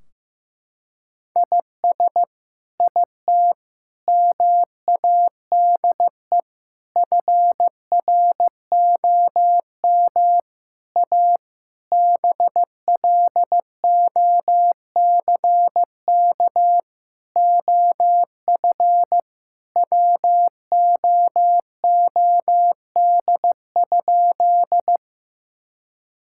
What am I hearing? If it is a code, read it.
Morse code, heard as a keyed tone, about 15 wpm, 700 Hz. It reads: IS IT MADE FROM A BLOCK OF WOOD?
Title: Sentences from Top 1600 Words 15wpm